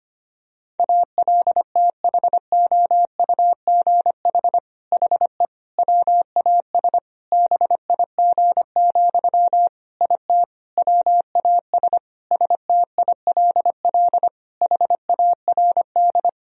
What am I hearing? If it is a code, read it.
Morse code, heard as a keyed tone, about 25 wpm, 700 Hz. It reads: ALTHOUGH HE WAS BIG, IT WAS STILL HARD
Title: Sentences from Top 1000 Words - Review Part 3 - 25wpm